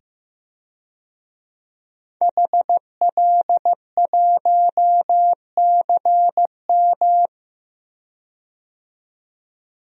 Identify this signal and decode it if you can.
Morse code, heard as a keyed tone, about 15 wpm, 700 Hz. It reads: HL1CM
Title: DXCC-100 Call Signs - 5 Characters - 15wpm